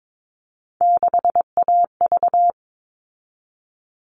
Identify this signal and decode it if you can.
Morse code, heard as a keyed tone, about 22 wpm, 700 Hz. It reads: 6AV
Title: All Cut Numbers - 3-Digit 22wpm